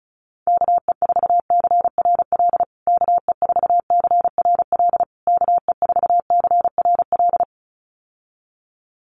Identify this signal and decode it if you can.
Morse code, heard as a keyed tone, about 34 wpm, 700 Hz. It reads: KE4CRL KE4CRL KE4CRL
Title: USA Call Signs Repeated 3x 35wpm